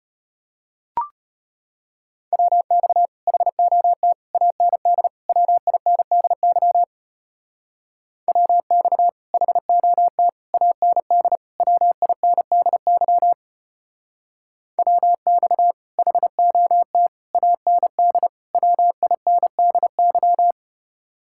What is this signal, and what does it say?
Morse code, heard as a keyed tone, about 30 wpm, 700 Hz. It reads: WX HOT AND WINDY WX HOT AND WINDY WX HOT AND WINDY
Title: Speed-Racing Common QSO Phrases Encore - 25wpm